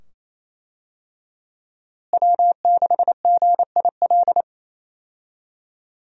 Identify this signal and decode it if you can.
Morse code, heard as a keyed tone, about 28 wpm, 700 Hz. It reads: W6GIL
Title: USA Call Signs - 5 Characters - Repeated 2x - 28wpm